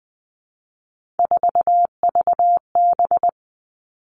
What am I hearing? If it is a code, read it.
Morse code, heard as a keyed tone, about 20 wpm, 700 Hz. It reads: 4VB
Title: Three Challenging Letter-Number Combinations 20wpm